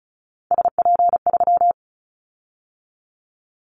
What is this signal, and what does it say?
Morse code, heard as a keyed tone, about 34 wpm, 700 Hz. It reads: SP3
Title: Callsign Prefixes for Top 25 DXCC - 35wpm